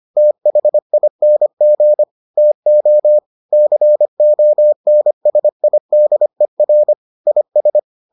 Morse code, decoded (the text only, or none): THING TO CONSIDER IS